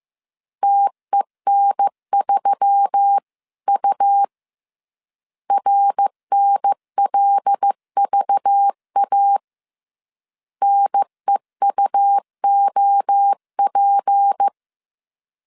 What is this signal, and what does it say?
Morse code, heard as a keyed tone, about 15 wpm, 800 Hz. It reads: TEN3U RNLVA NEUOP